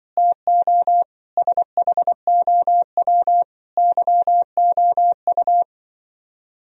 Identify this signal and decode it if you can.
Morse code, heard as a keyed tone, about 24 wpm, 700 Hz. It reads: TO SHOW YOU
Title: Speed-Racing - Sets of 5 Words 20wpm